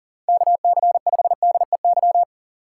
Morse code, heard as a keyed tone, about 40 wpm, 700 Hz. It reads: KC5DEY
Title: USA Call Signs - 6 Characters - 4x Character Spacing - 40wpm